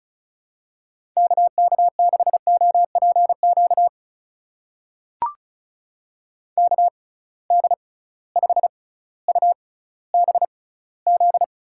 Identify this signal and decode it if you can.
Morse code, heard as a keyed tone, about 36 wpm, spaced as for 18 wpm, 700 Hz. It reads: KK6OPQ KD5UBZ
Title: USA Call Signs - 6 Characters - 6x Character Spacing - 35wpm